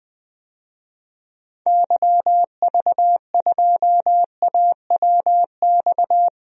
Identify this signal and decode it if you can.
Morse code, heard as a keyed tone, about 21 wpm, 700 Hz. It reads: YV2AWX
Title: DXCC-100 Call Signs - 6 Characters - Repeated 2x - 20wpm